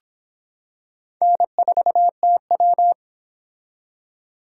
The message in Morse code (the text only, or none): N4TW